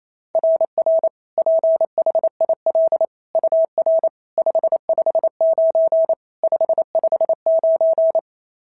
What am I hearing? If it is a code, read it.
Morse code, heard as a keyed tone, about 28 wpm, 690 Hz. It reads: RR PHIL UR 559 559
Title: SOTA QSO Practice 28wpm